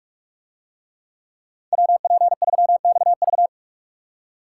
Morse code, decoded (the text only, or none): WP3XV